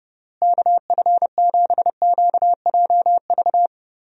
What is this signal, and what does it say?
Morse code, heard as a keyed tone, about 30 wpm, 700 Hz. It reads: KF7QJV
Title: USA Call Signs - 6 Characters - Repeated 3x - 30wpm